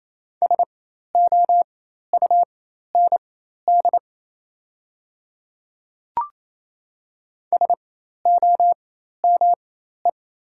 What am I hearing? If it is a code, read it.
Morse code, heard as a keyed tone, about 28 wpm, spaced as for 13 wpm, 700 Hz. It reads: SOUND SOME
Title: 4x Character Spacing - Top 100 Words - 28wpm